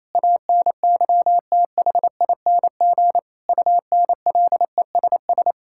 Morse code, heard as a keyed tone, about 28 wpm, 700 Hz. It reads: ANYTHING UNLESS
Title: Sentences from Top 1000 Words - Review Part 6 - 28wpm